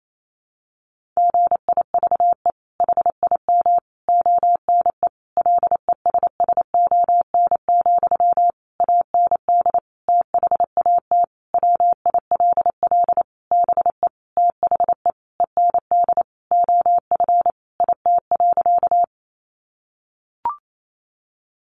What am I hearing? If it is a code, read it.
Morse code, heard as a keyed tone, about 28 wpm, 700 Hz. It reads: GIVE HIM ONE LESSON, AND THAT WILL BE THE END OF IT.
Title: Sentences from Top 1200 Words 28wpm